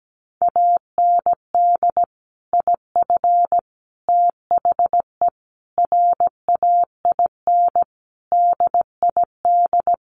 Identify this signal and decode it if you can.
Morse code, heard as a keyed tone, about 17 wpm, 700 Hz. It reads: AND IF THE RAIN DID
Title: Sentences from Top 1300 Words 17wpm